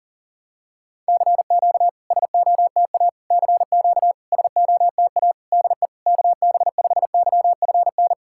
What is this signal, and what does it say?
Morse code, heard as a keyed tone, about 37 wpm, 690 Hz. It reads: CQ SOTA CQ SOTA DE KB5YFN
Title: SOTA QSO Practice 40wpm